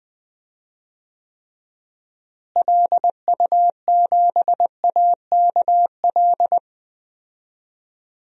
Morse code, heard as a keyed tone, about 20 wpm, 700 Hz. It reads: LU7AKL
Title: DXCC-100 Call Signs - 6 Characters - Repeated 3x - 20wpm